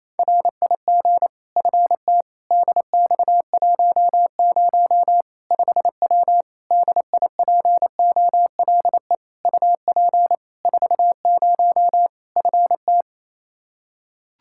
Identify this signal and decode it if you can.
Morse code, heard as a keyed tone, about 28 wpm, 700 Hz. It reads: RIG FT DX10 5W DIPOLE UP 40 FT
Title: Practice QSO - Exchange 2 - Info - 28wpm